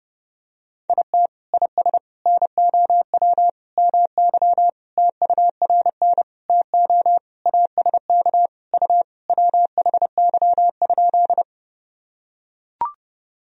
Morse code, heard as a keyed tone, about 30 wpm, 700 Hz. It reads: IT IS NOW MY TURN TO ASK U WHY?
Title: Sentences from Top 200 Words and Abbreviations 30wpm